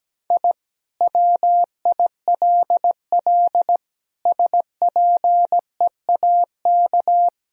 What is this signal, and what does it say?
Morse code, heard as a keyed tone, about 17 wpm, 700 Hz. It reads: I WILL SPEAK
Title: Sentences from Top 1000 Words - Review Part 2 - 17wpm